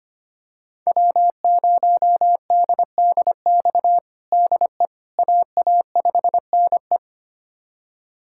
Morse code, heard as a keyed tone, about 25 wpm, 700 Hz. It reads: W0DDX DE AA5NE
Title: USA Callsign DE Callsign - 5 Characters - 25wpm